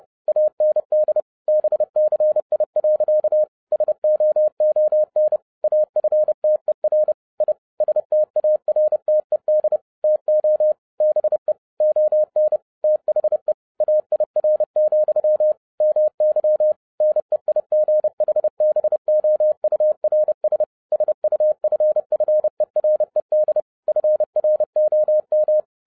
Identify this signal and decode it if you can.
Morse code, heard as a keyed tone, about 30 wpm, 610 Hz. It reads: AND BCI. SOON AFTER I STARTED TO BE ON THE AIR, MY NEIGHBOURS SUFFERED FROM
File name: 2011-07-mp3 / n40 EXCITING QRP CW OPERATION 30wpm